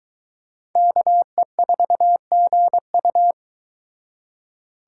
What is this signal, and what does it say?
Morse code, heard as a keyed tone, about 23 wpm, 700 Hz. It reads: KE4GU